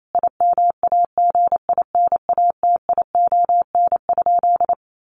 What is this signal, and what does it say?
Morse code, heard as a keyed tone, about 28 wpm, 700 Hz. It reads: IMAGINATION?
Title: Sentences from Top 1300 Words 28wpm